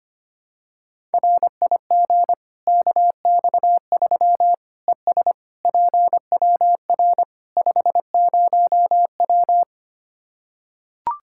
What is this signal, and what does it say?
Morse code, heard as a keyed tone, about 25 wpm, 700 Hz. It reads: RIG KX3 ES PWR 50W
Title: Practice QSO - Exchange 2 - Info - 25wpm